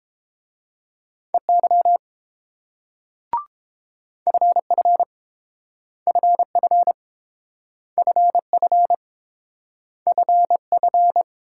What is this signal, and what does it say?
Morse code, heard as a keyed tone, about 30 wpm, 690 Hz. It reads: EY FF FF FF FF